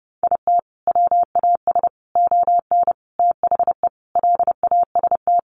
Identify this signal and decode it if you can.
Morse code, heard as a keyed tone, about 30 wpm, 700 Hz. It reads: IT WAS ON THE LAST
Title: Sentences from Top 1200 Words 30wpm